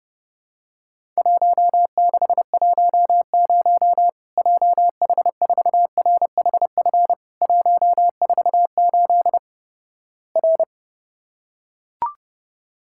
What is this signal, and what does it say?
Morse code, heard as a keyed tone, about 30 wpm, 700 Hz. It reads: 1610 JH4RHF 148 R
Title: Worked All Europe DX Contest - QTC Exchange - 30wpm